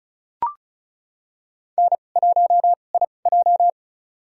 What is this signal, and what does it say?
Morse code, heard as a keyed tone, about 34 wpm, spaced as for 25 wpm, 690 Hz. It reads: N1IJ
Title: USA Call Signs - 4 Characters - 2x Character Spacing - 35wpm